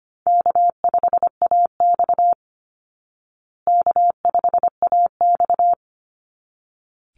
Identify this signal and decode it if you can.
Morse code, heard as a keyed tone, about 25 wpm, 700 Hz. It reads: K5AX K5AX
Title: Easy Call Signs Repeated 2x - 25wpm